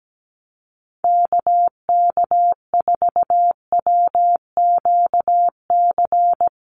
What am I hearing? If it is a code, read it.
Morse code, heard as a keyed tone, about 17 wpm, 700 Hz. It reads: KK4WQC
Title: USA Call Signs 17wpm